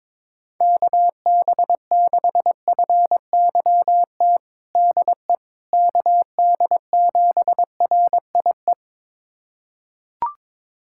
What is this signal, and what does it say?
Morse code, heard as a keyed tone, about 22 wpm, 700 Hz. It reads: KB6FYT DE KD7RIE
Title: USA Callsign DE Callsign - 6 Characters - 22wpm